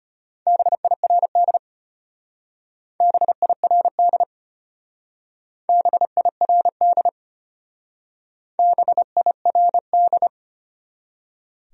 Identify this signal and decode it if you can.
Morse code, heard as a keyed tone, about 31 wpm, 700 Hz. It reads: BIRD BIRD BIRD BIRD